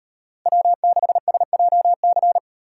Morse code, heard as a keyed tone, about 38 wpm, 700 Hz. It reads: W6SJC